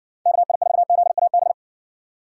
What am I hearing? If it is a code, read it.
Morse code, heard as a keyed tone, about 57 wpm, 700 Hz. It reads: KI4ZAB